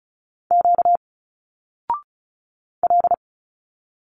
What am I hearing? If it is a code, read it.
Morse code, heard as a keyed tone, about 36 wpm, 700 Hz. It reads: Q L
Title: Single Letters 35wpm